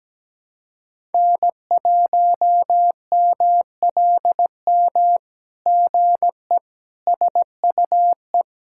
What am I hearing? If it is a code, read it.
Morse code, heard as a keyed tone, about 17 wpm, 700 Hz. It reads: N1MLM GE SUE
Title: SOTA QSO Practice 17wpm